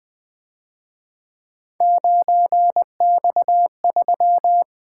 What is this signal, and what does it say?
Morse code, heard as a keyed tone, about 24 wpm, 700 Hz. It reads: 9X3
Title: Speed-Racing - Three Letter-Number Combinations - 20wpm